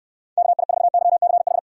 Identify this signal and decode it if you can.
Morse code, heard as a keyed tone, about 69 wpm, 700 Hz. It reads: KI4YCH